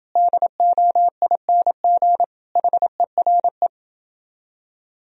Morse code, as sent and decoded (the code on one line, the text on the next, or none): -.. --- .. -. --. / .... .-. .
DOING HERE